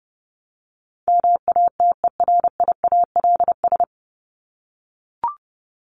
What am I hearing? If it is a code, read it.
Morse code, heard as a keyed tone, about 30 wpm, 700 Hz. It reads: MATERIALS
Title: ES:S Suffix - Top 100 Words with Base Words 30wpm